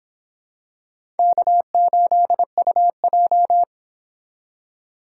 Morse code, heard as a keyed tone, about 26 wpm, 700 Hz. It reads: K8UJ